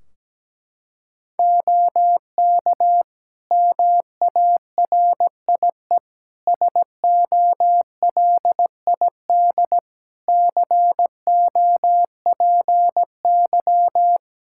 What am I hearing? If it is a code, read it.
Morse code, heard as a keyed tone, about 17 wpm, 700 Hz. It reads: OK MARIE SOLID COPY
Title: Practice QSO - Exchange 3 - More Info - 17wpm